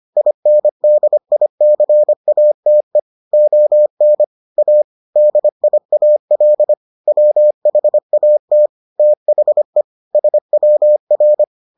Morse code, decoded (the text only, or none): INDICATE ON A DIAL WHAT THE SWR